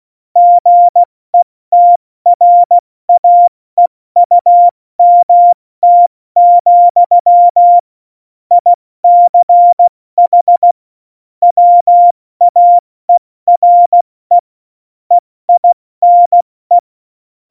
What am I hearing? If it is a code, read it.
Morse code, heard as a keyed tone, about 16 wpm, 700 Hz. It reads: GETRAEUMT, ICH WAERE EINE
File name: Morsekurs / Witze / Witz 26. Dezember 2017, 70 F80